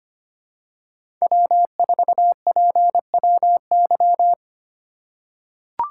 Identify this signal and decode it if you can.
Morse code, heard as a keyed tone, about 25 wpm, 700 Hz. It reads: W4PWY
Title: USA Call Signs - 5 Characters - 2x Character Spacing - 25wpm